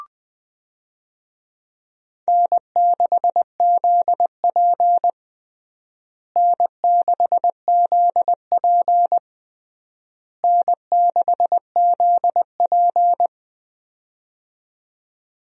Morse code, heard as a keyed tone, about 20 wpm, 700 Hz. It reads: N6ZP N6ZP N6ZP
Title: USA Call Signs - 4 Characters - Repeated 3x - 20wpm